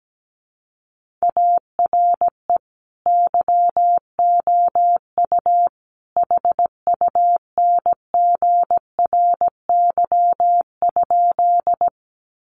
Morse code, heard as a keyed tone, about 17 wpm, 700 Hz. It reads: ARE YOU HUNGRY?
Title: Sentences from Top 1400 Words 17wpm